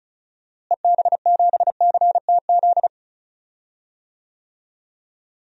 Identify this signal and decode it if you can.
Morse code, heard as a keyed tone, about 34 wpm, 700 Hz. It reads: EB7CTZ
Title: DXCC-100 Call Signs - 6 Characters - Repeated 3x - 35wpm